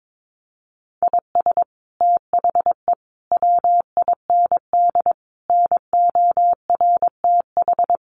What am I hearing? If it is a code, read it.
Morse code, heard as a keyed tone, about 22 wpm, 700 Hz. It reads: IS THE WIND NORTH
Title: Sentences from Top 400 Words 22wpm